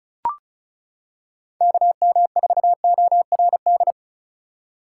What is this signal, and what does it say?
Morse code, heard as a keyed tone, about 36 wpm, 700 Hz. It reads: KM4ORD
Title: K1USN SST QSO Practice - 35wpm